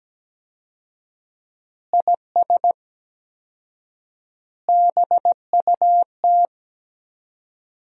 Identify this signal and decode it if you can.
Morse code, heard as a keyed tone, about 17 wpm, 700 Hz. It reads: IS BUT
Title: 4x Word-Spacing - Sentences from Top 100 Words - 17wpm